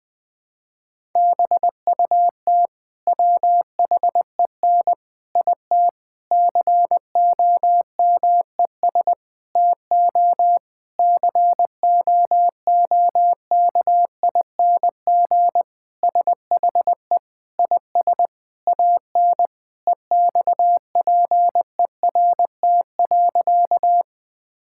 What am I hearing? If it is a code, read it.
Morse code, heard as a keyed tone, about 21 wpm, 700 Hz. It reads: BUT WHEN IT COMES TO COOKING SHE IS AN EXPERT.
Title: Sentences with 5-Word Phrases - 20wpm